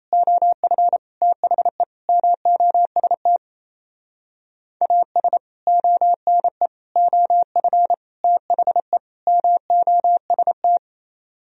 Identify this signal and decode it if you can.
Morse code, heard as a keyed tone, about 35 wpm, 700 Hz. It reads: OF THE MOST AS ONE OF THE MOST
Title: Speed-Racing - Sets of 5 Words 28wpm